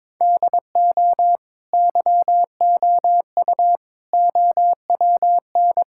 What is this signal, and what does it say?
Morse code, heard as a keyed tone, about 22 wpm, 700 Hz. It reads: DO YOU OWN
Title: Sentences from Top 1000 Words - Review Part 6 - 22wpm